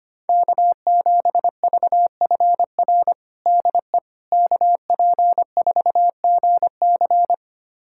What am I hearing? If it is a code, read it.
Morse code, heard as a keyed tone, about 25 wpm, 700 Hz. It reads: K7VFR DE KP4GC
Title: USA Callsign DE Callsign - 5 Characters - 25wpm